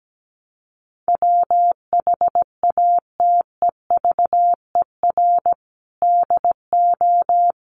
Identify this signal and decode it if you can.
Morse code, heard as a keyed tone, about 17 wpm, 700 Hz. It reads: WHATEVER DO
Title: Sentences from Top 1100 Words 17wpm